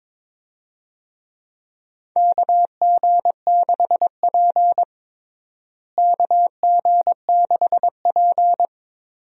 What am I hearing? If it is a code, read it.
Morse code, heard as a keyed tone, about 22 wpm, 700 Hz. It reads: KG6P KG6P
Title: USA Call Signs - 4 Characters - Repeated 2x - 22wpm